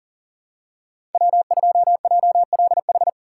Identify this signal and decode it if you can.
Morse code, heard as a keyed tone, about 40 wpm, 690 Hz. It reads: W2JLH
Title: USA Call Signs - 5 Characters - 6x Character Spacing - 40wpm